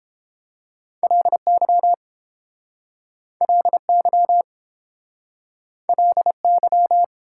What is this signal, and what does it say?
Morse code, heard as a keyed tone, about 31 wpm, 700 Hz. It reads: LY LY LY